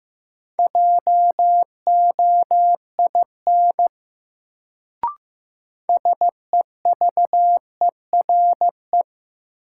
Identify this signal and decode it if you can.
Morse code, heard as a keyed tone, about 15 wpm, 700 Hz. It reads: JOIN SEVERE
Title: Sentences from Top 1400 Words - New Word Drill Alt - 15wpm